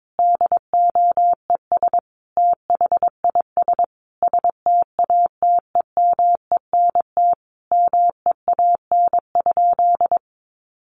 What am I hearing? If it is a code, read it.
Morse code, heard as a keyed tone, about 22 wpm, 700 Hz. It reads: DOES THIS STATEMENT MEAN?
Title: Sentences from Top 1200 Words 22wpm